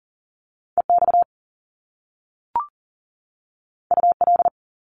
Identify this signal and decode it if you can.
Morse code, heard as a keyed tone, about 40 wpm, 700 Hz. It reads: EX UL
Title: Top Two Letter Combinations in Words 40wpm